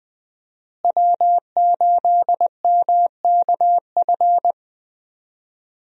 Morse code, heard as a keyed tone, about 20 wpm, 700 Hz. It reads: W8MKF
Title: USA Call Signs - 5 Characters - Repeated 2x - 20wpm